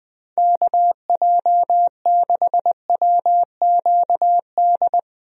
Morse code, heard as a keyed tone, about 20 wpm, 700 Hz. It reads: KJ6WQD
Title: USA Call Signs - 6 Characters - 20wpm